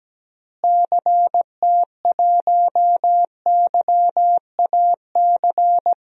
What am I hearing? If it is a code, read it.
Morse code, heard as a keyed tone, about 17 wpm, 700 Hz. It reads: CT1YAC
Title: DXCC-100 Call Signs - 6 Characters - Repeated 2x - 17wpm